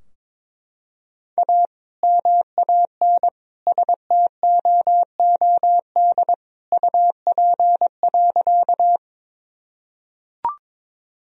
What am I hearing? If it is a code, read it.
Morse code, heard as a keyed tone, about 22 wpm, 700 Hz. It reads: A MAN STOOD UP.
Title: Sentences from Top 1000 Words - Review Part 2 - 22wpm